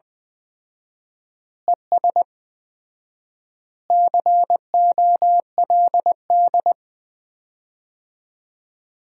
Morse code, heard as a keyed tone, about 21 wpm, 700 Hz. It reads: ES COLD
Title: Common QSO Phrases - 4x Word-Spacing - 20wpm